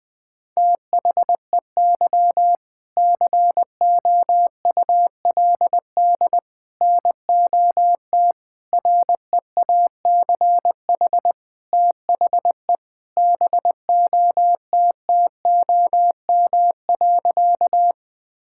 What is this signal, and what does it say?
Morse code, heard as a keyed tone, about 20 wpm, 700 Hz. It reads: THEY COULD NOT REACH THE BOTTOM.
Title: Sentences from Top 1000 Words - Review Part 6 - 20wpm